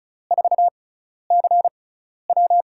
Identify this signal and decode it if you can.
Morse code, heard as a keyed tone, about 36 wpm, spaced as for 12 wpm, 700 Hz. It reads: 4CW
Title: USA Call Signs - 5 Characters - 6x Character Spacing - 35wpm